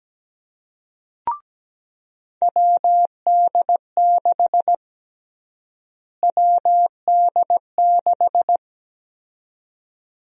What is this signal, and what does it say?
Morse code, heard as a keyed tone, about 17 wpm, 700 Hz. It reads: WD6 WD6
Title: USA Call Sign Prefixes - Repeated 2x - 17wpm